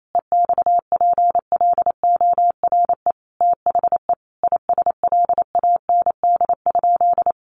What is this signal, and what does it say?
Morse code, heard as a keyed tone, about 28 wpm, 700 Hz. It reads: EXPLORE THE ISLAND?
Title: Sentences from Top 1100 Words 28wpm